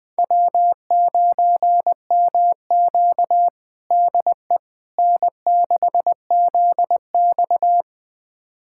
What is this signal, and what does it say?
Morse code, heard as a keyed tone, about 20 wpm, 700 Hz. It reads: W9MQ DE N6ZX
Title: USA Callsign DE Callsign - 4 Characters - 20wpm